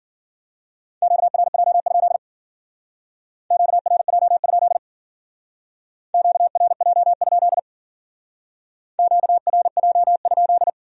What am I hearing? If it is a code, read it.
Morse code, heard as a keyed tone, about 49 wpm, 700 Hz. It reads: QRJ? QRJ? QRJ? QRJ?
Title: Speed-Racing Q-Codes - 40wpm